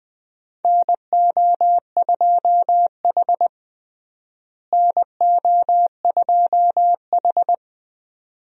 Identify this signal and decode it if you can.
Morse code, heard as a keyed tone, about 20 wpm, 700 Hz. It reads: NO2H NO2H
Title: USA Call Signs - 4 Characters - Repeated 2x - 20wpm